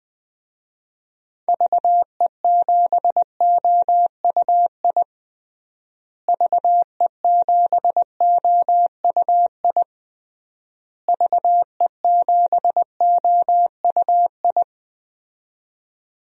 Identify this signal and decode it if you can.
Morse code, heard as a keyed tone, about 21 wpm, 700 Hz. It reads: VE7OUI VE7OUI VE7OUI
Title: DXCC-100 Call Signs - 6 Characters - Repeated 3x - 20wpm